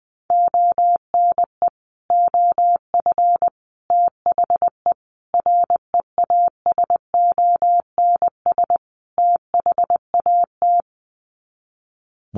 Morse code, heard as a keyed tone, about 19 wpm, 700 Hz. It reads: ONE OF THE REASONS THAT
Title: Sets of 5 Words Encore 20wpm